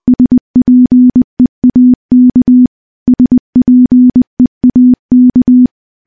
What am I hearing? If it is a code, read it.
Morse code, heard as a keyed tone, about 21 wpm, 250 Hz. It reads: SPEAK SPEAK
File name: morse-training-20WPM